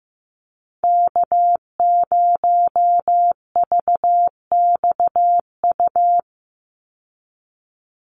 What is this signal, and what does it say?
Morse code, heard as a keyed tone, about 15 wpm, 700 Hz. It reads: K0VXU
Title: Call Signs Repeated 3x - 15wpm